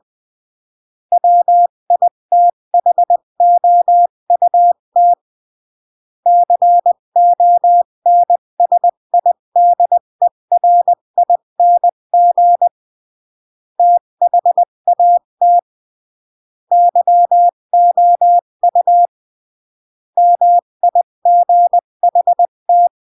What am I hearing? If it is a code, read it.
Morse code, heard as a keyed tone, about 20 wpm, 700 Hz. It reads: WITHOUT CONSIDERING THAT YOU MIGHT
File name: CW / jodorowsky / 82 commandments 20-15